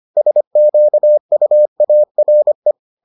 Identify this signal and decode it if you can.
Morse code, heard as a keyed tone, about 25 wpm, 600 Hz. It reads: SQUARE